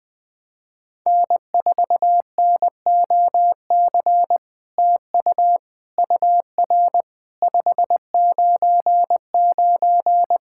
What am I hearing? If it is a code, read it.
Morse code, heard as a keyed tone, about 20 wpm, 700 Hz. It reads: N4NOC TU UR 599
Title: SOTA QSO Practice 20wpm